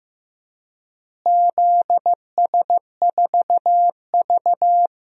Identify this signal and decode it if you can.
Morse code, heard as a keyed tone, about 15 wpm, 700 Hz. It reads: ZS4V
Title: DXCC-100 Call Signs - 4 Characters - Repeated 3x - 15wpm